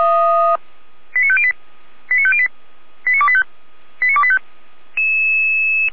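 Five-tone selective call (ZVEI), heard as quick runs of five tones, tones of about 70 ms. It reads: 89508 89508 89285 89285
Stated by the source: ZVEI five-tone selective call 89025, received by SDR.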